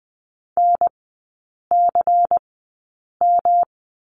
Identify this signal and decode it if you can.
Morse code, heard as a keyed tone, about 20 wpm, spaced as for 8 wpm, 700 Hz. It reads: NCM
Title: 172 Lesson 28e - Number 9 - Three Characters 20wpm